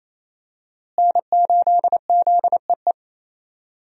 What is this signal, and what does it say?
Morse code, heard as a keyed tone, about 28 wpm, 700 Hz. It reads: N8ZEE
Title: USA Call Signs - 5 Characters - Repeated 3x - 28wpm